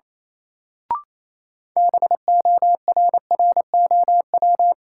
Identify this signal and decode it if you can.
Morse code, heard as a keyed tone, about 27 wpm, 700 Hz. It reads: BORROW
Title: Sentences from Top 1500 Words - New Word Drill Alt - 28wpm